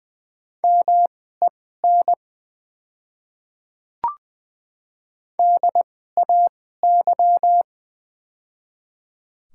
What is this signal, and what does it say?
Morse code, heard as a keyed tone, about 20 wpm, spaced as for 14 wpm, 700 Hz. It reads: MEN DAY
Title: 2x Character Spacing - Top 200 Words - 20wpm